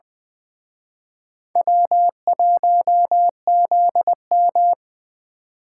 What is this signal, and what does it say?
Morse code, heard as a keyed tone, about 24 wpm, 700 Hz. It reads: W1ZM
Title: Speed Racing - USA 1x2 Call Signs 20wpm